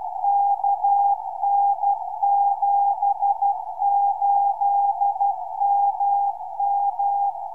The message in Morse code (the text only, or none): KN78MM